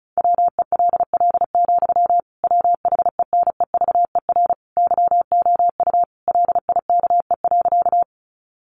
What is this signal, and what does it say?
Morse code, heard as a keyed tone, about 34 wpm, 700 Hz. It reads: WELL, WHENEVER YOU LIKE.
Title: Sentences from Top 1200 Words 35wpm